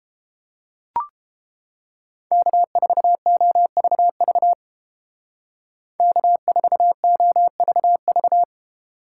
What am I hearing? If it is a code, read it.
Morse code, heard as a keyed tone, about 28 wpm, 700 Hz. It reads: K4OVV K4OVV